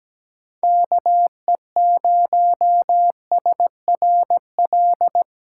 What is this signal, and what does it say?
Morse code, heard as a keyed tone, about 17 wpm, 700 Hz. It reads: KE0SRL
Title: K1USN SST QSO Practice - 17wpm